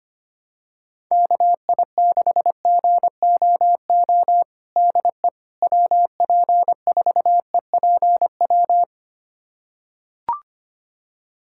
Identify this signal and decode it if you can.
Morse code, heard as a keyed tone, about 25 wpm, 700 Hz. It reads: KI6GOO DE WP4EPW
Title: USA Callsign DE Callsign - 6 Characters - 25wpm